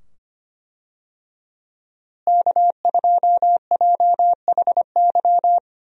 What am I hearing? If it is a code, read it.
Morse code, heard as a keyed tone, about 25 wpm, 700 Hz. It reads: K2JHY
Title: USA Call Signs - 5 Characters - Repeated 2x - 25wpm